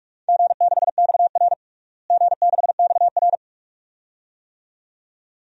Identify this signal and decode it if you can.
Morse code, heard as a keyed tone, about 46 wpm, 700 Hz. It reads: G6XR G6XR
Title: DXCC-100 Call Signs - 4 Characters - Repeated 2x - 45wpm